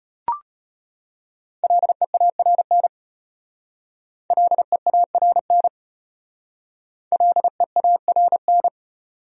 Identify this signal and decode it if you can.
Morse code, heard as a keyed tone, about 31 wpm, 700 Hz. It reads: LEARN LEARN LEARN
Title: Speed-Racing Top 300 Words 25wpm